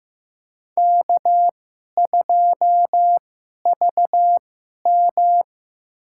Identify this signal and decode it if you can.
Morse code, heard as a keyed tone, about 15 wpm, spaced as for 11 wpm, 700 Hz. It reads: K2VM
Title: USA Call Signs - 4 Characters - 2x Character Spacing - 15wpm